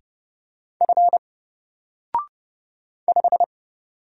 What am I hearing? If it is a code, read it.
Morse code, heard as a keyed tone, about 31 wpm, 700 Hz. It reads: F 5
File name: Single-Letter-Number-30wpm